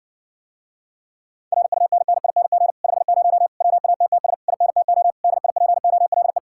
Morse code, heard as a keyed tone, about 60 wpm, 700 Hz. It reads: RUNNING 50 WATTS INTO DIPOLE